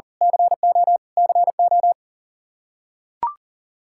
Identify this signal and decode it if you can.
Morse code, heard as a keyed tone, about 40 wpm, 700 Hz. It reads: CO CO
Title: K1USN SST QSO Practice 40wpm